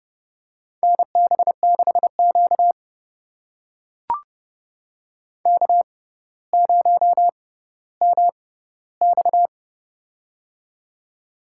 Morse code, heard as a keyed tone, about 30 wpm, spaced as for 15 wpm, 700 Hz. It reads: NB6Q K0MX
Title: USA Call Signs - 4 Characters - 6x Character Spacing - 30wpm